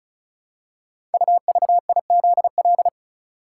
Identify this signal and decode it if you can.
Morse code, heard as a keyed tone, about 36 wpm, 700 Hz. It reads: UVIZL
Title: Code Groups - 5 Letters - 35wpm